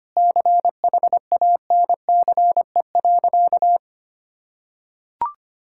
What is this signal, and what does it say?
Morse code, heard as a keyed tone, about 25 wpm, 700 Hz. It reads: CHANCE.
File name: practice / 25wpm / Sentences with 2-Word Phrases - 25wpm